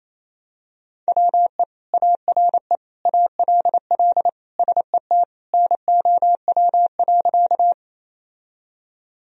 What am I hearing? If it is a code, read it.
Morse code, heard as a keyed tone, about 28 wpm, 700 Hz. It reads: WE ARE ALL SET NOW.
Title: Sentences from Top 1000 Words - Review Part 1 - 28wpm